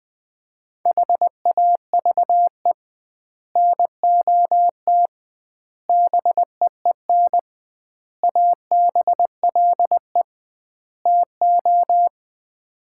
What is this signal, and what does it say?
Morse code, heard as a keyed tone, about 21 wpm, 700 Hz. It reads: HAVE NOT BEEN ABLE TO
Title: 2x Word-Spacing - Sets of 5 Words - 20wpm